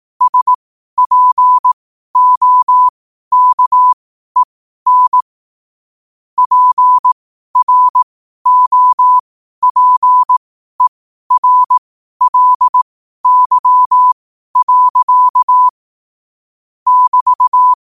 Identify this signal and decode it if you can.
Morse code, heard as a keyed tone, about 18 wpm, spaced as for 12 wpm, 1000 Hz. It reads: SPOKEN PROPERLY. <BT>
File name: qotdfiles / 12wpm